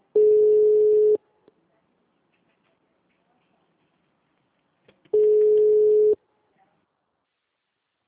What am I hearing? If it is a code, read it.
Ringback tone (Europe, 425 Hz).